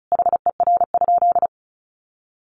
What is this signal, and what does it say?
Morse code, heard as a keyed tone, about 34 wpm, 700 Hz. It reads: HER?
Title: Sentences from Top 400 Words 35wpm